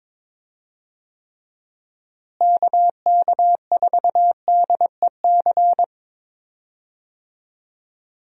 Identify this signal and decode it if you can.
Morse code, heard as a keyed tone, about 22 wpm, 700 Hz. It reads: KK4DEC
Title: USA Call Signs - 6 Characters - Repeated 2x - 22wpm